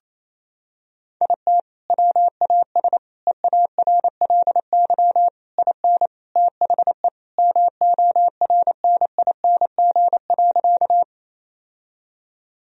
Morse code, heard as a keyed tone, about 28 wpm, 700 Hz. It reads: IT WAS EARLY IN THE MORNING.
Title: Sentences from Top 1000 Words - Review Part 2 - 28wpm